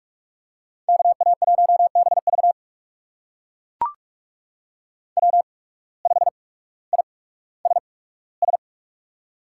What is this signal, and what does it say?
Morse code, heard as a keyed tone, about 43 wpm, spaced as for 19 wpm, 700 Hz. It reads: KA1BV W5ISS
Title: USA Call Signs - 8x Character Spacing - 45wpm